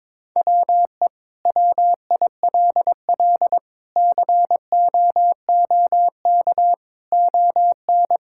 Morse code, heard as a keyed tone, about 22 wpm, 700 Hz. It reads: WE WILL COOK ON
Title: Sentences from Top 1000 Words - Review Part 6 - 22wpm